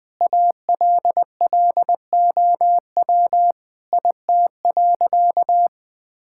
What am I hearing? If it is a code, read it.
Morse code, heard as a keyed tone, about 20 wpm, 700 Hz. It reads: ALLOW IT.
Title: Sentences from Top 1000 Words - Review Part 5 - 20wpm